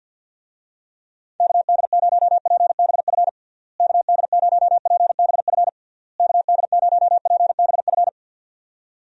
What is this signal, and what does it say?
Morse code, heard as a keyed tone, about 51 wpm, 700 Hz. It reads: KD0PBF KD0PBF KD0PBF
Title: USA Call Signs - 6 Characters - Repeated 3x - 50wpm